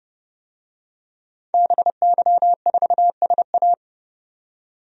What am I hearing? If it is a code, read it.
Morse code, heard as a keyed tone, about 31 wpm, 700 Hz. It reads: BY4SA
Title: DXCC-100 Call Signs - 5 Characters - 30wpm